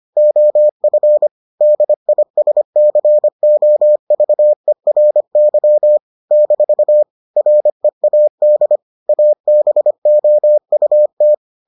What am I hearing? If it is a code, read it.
Morse code, heard as a keyed tone, about 25 wpm, 600 Hz. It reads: OF DISCOVERY - READ ABOUT